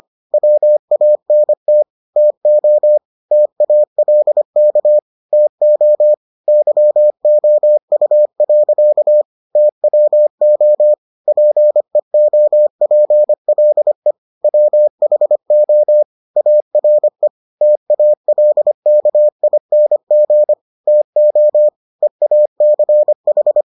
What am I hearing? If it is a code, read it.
Morse code, heard as a keyed tone, about 25 wpm, 600 Hz. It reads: WANT TO TALK TO YOU. TWO PEOPLE WHO ARE TALKING TO EACH